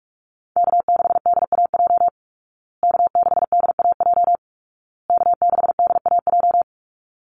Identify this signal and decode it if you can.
Morse code, heard as a keyed tone, about 44 wpm, 700 Hz. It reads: K6DAJ K6DAJ K6DAJ
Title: Call Signs Repeated 2x - 45wpm